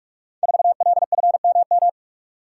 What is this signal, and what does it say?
Morse code, heard as a keyed tone, about 44 wpm, 700 Hz. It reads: 4LFMM
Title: Code Groups - 5 Letter:Numbers - 45wpm